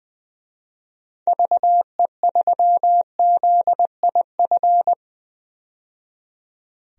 Morse code, heard as a keyed tone, about 21 wpm, 700 Hz. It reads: VE3ZIF
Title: DXCC-100 Call Signs - 6 Characters - 20wpm